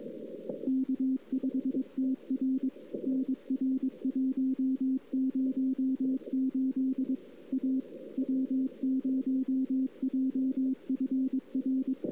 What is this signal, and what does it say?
Morse code, heard as a keyed tone, about 22 wpm, 260 Hz. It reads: K5TR NR108 A W0JFR
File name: CW / 2008-ss-cw-k5tr-op-n5rz -03:00